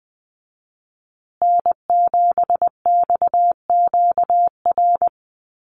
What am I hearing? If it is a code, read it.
Morse code, heard as a keyed tone, about 20 wpm, 700 Hz. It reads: N7XQR
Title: USA Call Signs Repeated 2x 20wpm